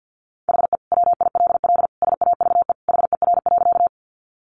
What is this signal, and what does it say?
Morse code, heard as a keyed tone, about 50 wpm, 700 Hz. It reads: HE WILL SAVE HER.
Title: Sentences from Top 1100 Words 50wpm